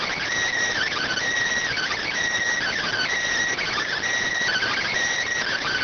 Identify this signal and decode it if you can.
Emergency Alert System (SAME) tones, heard as harsh buzzing bursts. It is heard from 0.8 to 4.9 s.